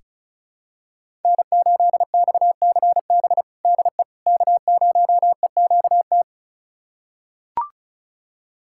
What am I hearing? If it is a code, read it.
Morse code, heard as a keyed tone, about 36 wpm, 700 Hz. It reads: N8XCB DE K0EQT
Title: USA Callsign DE Callsign - 5 Characters - 35wpm